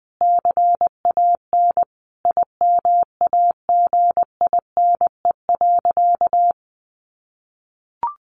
Morse code, heard as a keyed tone, about 20 wpm, 700 Hz. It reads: CAN IMAGINE.
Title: Sentences from Top 1500 Words 20wpm